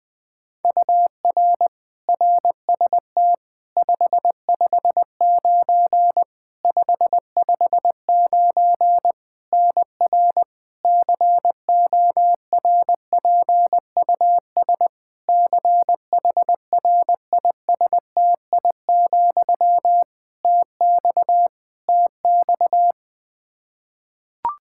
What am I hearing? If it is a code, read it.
Morse code, heard as a keyed tone, about 20 wpm, 700 Hz. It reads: UR RST 559 559 NR CORPUS CHRISTI, TX TX
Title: K1USN SST QSO Practice 20wpm